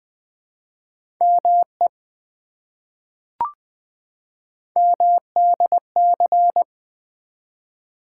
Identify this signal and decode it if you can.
Morse code, heard as a keyed tone, about 20 wpm, 700 Hz. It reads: ME MDC
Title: ARRL Sections v2 - 20wpm